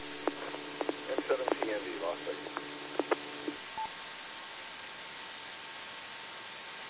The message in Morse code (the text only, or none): none